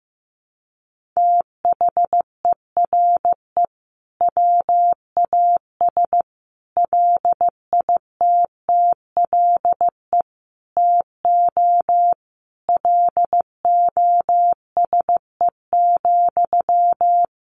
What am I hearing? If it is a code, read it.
Morse code, heard as a keyed tone, about 15 wpm, 700 Hz. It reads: THERE WAS LITTLE TO LOSE,
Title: Sentences from Top 1100 Words 15wpm